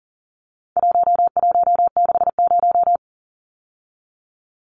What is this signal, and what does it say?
Morse code, heard as a keyed tone, about 40 wpm, 700 Hz. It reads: 1160
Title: Numbers - 4-Digit 40wpm